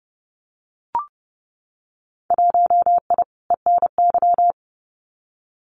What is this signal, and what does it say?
Morse code, heard as a keyed tone, about 30 wpm, 700 Hz. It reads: 1I ENY
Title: Winter Field Day Exchanges Encore 30wpm